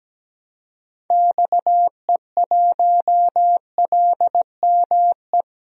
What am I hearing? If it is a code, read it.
Morse code, heard as a keyed tone, about 17 wpm, 700 Hz. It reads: XE1LME
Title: DXCC-100 Call Signs - 6 Characters - 17wpm